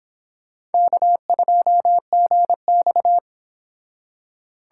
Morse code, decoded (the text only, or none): K2GX